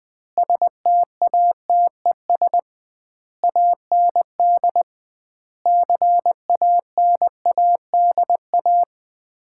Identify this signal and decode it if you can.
Morse code, heard as a keyed tone, about 20 wpm, 700 Hz. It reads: STATES AND CANADA
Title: 2x Word-Spacing - Sets of 5 Words - 20wpm